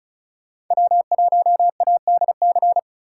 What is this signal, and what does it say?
Morse code, heard as a keyed tone, about 37 wpm, 700 Hz. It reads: W1ADC
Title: Speed Racing - USA 1x3 Call Signs 30wpm